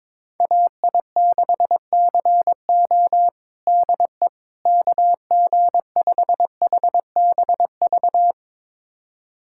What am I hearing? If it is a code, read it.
Morse code, heard as a keyed tone, about 22 wpm, 700 Hz. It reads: AI6CO DE KG5HBV
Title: USA Callsign DE Callsign 22wpm